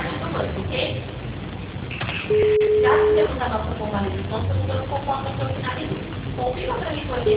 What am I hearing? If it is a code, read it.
Ringback tone (Europe, 425 Hz), with speech.
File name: out- -2007- - - .43436